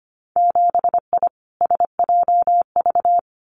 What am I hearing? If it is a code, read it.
Morse code, heard as a keyed tone, about 25 wpm, 700 Hz. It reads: 7I SJV
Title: Winter Field Day Exchanges Encore 25wpm